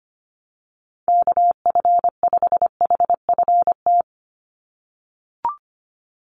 Morse code, heard as a keyed tone, about 25 wpm, 700 Hz. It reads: KF5HFT
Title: USA Call Signs 25wpm